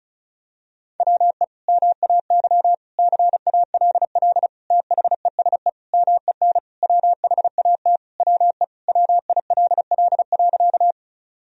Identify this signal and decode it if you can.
Morse code, heard as a keyed tone, about 36 wpm, 700 Hz. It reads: WE MAY CALL THESE MEN WHAT WE WILL.
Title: Sentences from Top 1000 Words - Review Part 1 - 35wpm